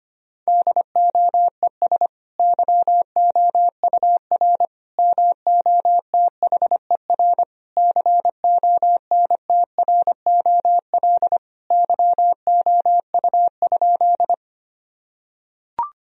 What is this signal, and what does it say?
Morse code, heard as a keyed tone, about 25 wpm, 700 Hz. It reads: DOES YOUR MOTHER CONTROL YOU?
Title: Sentences from Top 1000 Words - Review Part 3 - 25wpm